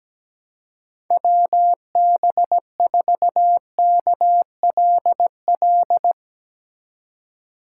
Morse code, heard as a keyed tone, about 17 wpm, 700 Hz. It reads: WB4KLL
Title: USA Call Signs - 6 Characters - 4x Character Spacing - 17wpm